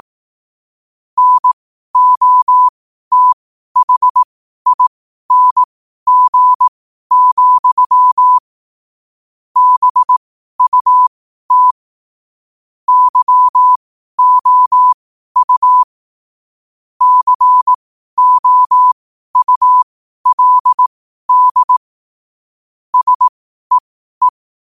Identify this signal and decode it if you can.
Morse code, heard as a keyed tone, about 18 wpm, spaced as for 12 wpm, 1000 Hz. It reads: NOTHING, BUT YOU COULD SEE
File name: qotdfiles / 12wpm